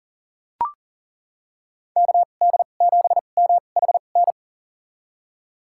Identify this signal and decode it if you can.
Morse code, heard as a keyed tone, about 40 wpm, spaced as for 29 wpm, 700 Hz. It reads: KD7MHN
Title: USA Call Signs - 6 Characters - 2x Character Spacing - 40wpm